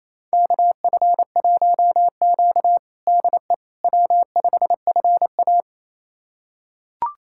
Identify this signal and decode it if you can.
Morse code, heard as a keyed tone, about 28 wpm, 700 Hz. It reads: KF1Q DE W5FA
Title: USA Callsign DE Callsign - 4 Characters - 28wpm